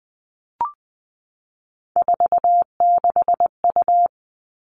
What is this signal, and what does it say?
Morse code, heard as a keyed tone, about 20 wpm, 700 Hz. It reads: 46U
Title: All Cut Numbers - 3-Digit 20wpm